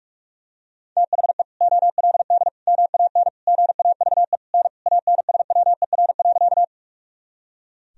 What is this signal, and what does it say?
Morse code, heard as a keyed tone, about 41 wpm, 700 Hz. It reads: THE OLD MAN GAVE N ANSWER.
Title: Sentences from Top 500 Words and Abbreviations 45wpm